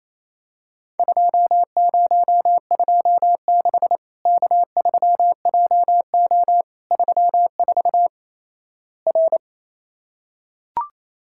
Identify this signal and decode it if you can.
Morse code, heard as a keyed tone, about 28 wpm, 700 Hz. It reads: 2026 K3JO 34 R
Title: Worked All Europe DX Contest - QTC Exchange - 28wpm